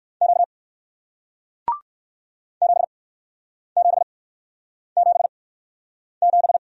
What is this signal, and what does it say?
Morse code, heard as a keyed tone, about 56 wpm, 700 Hz. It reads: Q 7 7 7 7
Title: Speed-Racing Single Letter-Number 45wpm